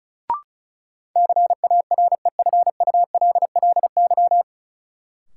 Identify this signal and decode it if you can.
Morse code, heard as a keyed tone, about 34 wpm, 700 Hz. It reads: CAREFULLY
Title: Sentences from Top 1300 Words - New Word Drill Alt - 35wpm